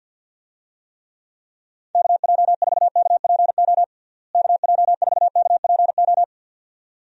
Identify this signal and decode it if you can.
Morse code, heard as a keyed tone, about 51 wpm, 700 Hz. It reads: KJ4KPO KJ4KPO
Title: USA Call Signs - 6 Characters - Repeated 2x - 50wpm